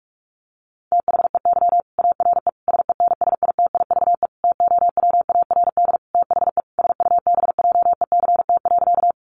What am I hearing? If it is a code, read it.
Morse code, heard as a keyed tone, about 44 wpm, 700 Hz. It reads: THEY ARE SENSITIVE TOWARD THE SUBJECT.
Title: Sentences from Top 1400 Words 45wpm